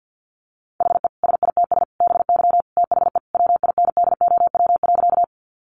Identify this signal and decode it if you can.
Morse code, heard as a keyed tone, about 50 wpm, 700 Hz. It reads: HE SITS BY THE WINDOW.
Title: Easy Phrases 50wpm